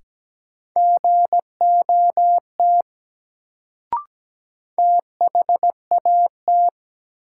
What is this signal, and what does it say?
Morse code, heard as a keyed tone, about 17 wpm, 700 Hz. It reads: GOT THAT
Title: Sets of 2 Words - New Word Drill - 17wpm